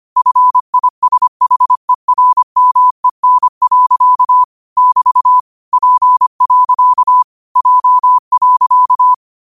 Morse code, heard as a keyed tone, about 25 wpm, 1000 Hz. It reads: FISHERMEN. <BT> P. J.
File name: qotdfiles / 25wpm